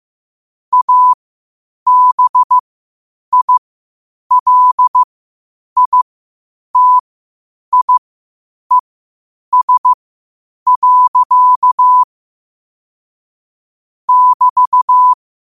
Morse code, heard as a keyed tone, about 15 wpm, spaced as for 8 wpm, 1000 Hz. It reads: ABILITIES. <BT>
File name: qotdfiles / 8wpm